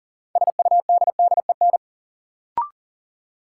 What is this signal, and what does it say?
Morse code, heard as a keyed tone, about 40 wpm, 700 Hz. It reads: SUDDEN
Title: Sets of 4 Words - New Word Drill - 40wpm